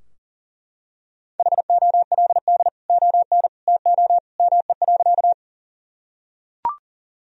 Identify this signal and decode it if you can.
Morse code, heard as a keyed tone, about 37 wpm, 700 Hz. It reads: HOLD ON TO ME.
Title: Sentences from Top 1000 Words - Review Part 2 - 40wpm